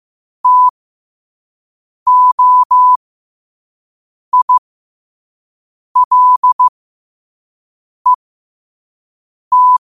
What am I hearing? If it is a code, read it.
Morse code, heard as a keyed tone, about 15 wpm, spaced as for 5 wpm, 1000 Hz. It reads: TOILET
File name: qotdfiles / 5wpm